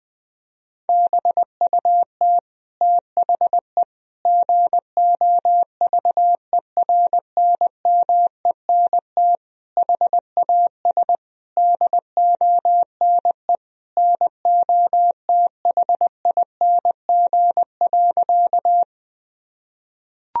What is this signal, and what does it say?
Morse code, heard as a keyed tone, about 20 wpm, 700 Hz. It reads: BUT THE GOVERNMENT HAS DONE NOTHING.
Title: Sentences from Top 1000 Words - Review Part 3 - 20wpm